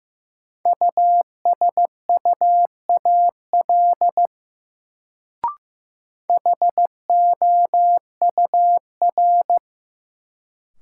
Morse code, heard as a keyed tone, about 15 wpm, 700 Hz. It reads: USUAL HOUR
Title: Sentences from Top 400 Words - New Word Drill - 15wpm